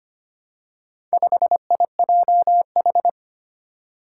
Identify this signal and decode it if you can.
Morse code, heard as a keyed tone, about 25 wpm, 700 Hz. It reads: 5IJH
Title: Four Challenging Letter-Number Combinations 25wpm